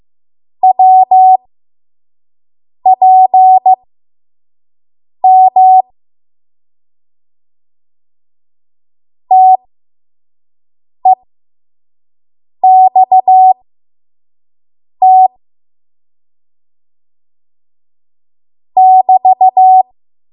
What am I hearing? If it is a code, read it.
Morse code, heard as a keyed tone, about 15 wpm, spaced as for 5 wpm, 750 Hz. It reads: WPM TEXT <BT>
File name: Morse / 05WPM